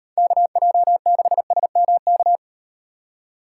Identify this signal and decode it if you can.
Morse code, heard as a keyed tone, about 38 wpm, 700 Hz. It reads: KJ6SMK